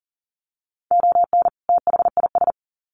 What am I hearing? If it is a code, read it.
Morse code, heard as a keyed tone, about 40 wpm, 700 Hz. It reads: ON THIS
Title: Sets of 2 Words 40wpm